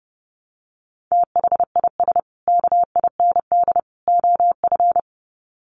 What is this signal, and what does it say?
Morse code, heard as a keyed tone, about 30 wpm, 700 Hz. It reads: THIS KIND OF
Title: Sets of 3 Words 30wpm